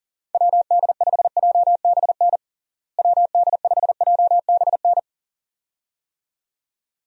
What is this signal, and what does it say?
Morse code, heard as a keyed tone, about 37 wpm, 700 Hz. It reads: WD5JBN WD5JBN
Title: USA Call Signs - 6 Characters - Repeated 2x - 40wpm